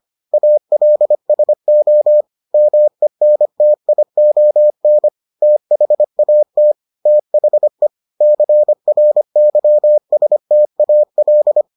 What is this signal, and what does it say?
Morse code, heard as a keyed tone, about 25 wpm, 600 Hz. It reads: ALSO MENTION THAT THE CRYSTAL